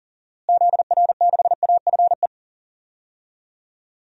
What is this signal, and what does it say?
Morse code, heard as a keyed tone, about 40 wpm, 700 Hz. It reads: ZR6AFE
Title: DXCC-100 Call Signs - 6 Characters - Repeated 2x - 40wpm